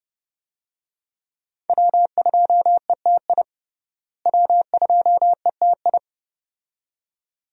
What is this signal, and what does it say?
Morse code, heard as a keyed tone, about 30 wpm, 700 Hz. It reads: W2ETI W2ETI
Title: USA Call Signs - 5 Characters - Repeated 2x - 30wpm